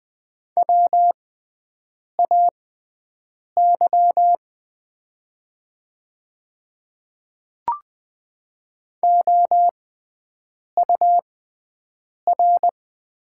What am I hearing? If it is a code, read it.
Morse code, heard as a keyed tone, about 20 wpm, spaced as for 7 wpm, 700 Hz. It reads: WAY OUR
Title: 6x Character Spacing - Top 200 Words - 20wpm